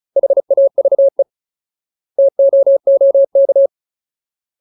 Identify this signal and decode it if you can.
Morse code, heard as a keyed tone, about 34 wpm, 550 Hz. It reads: HAVE TOOK
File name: Ultimate Word List / 35WPM Ultimate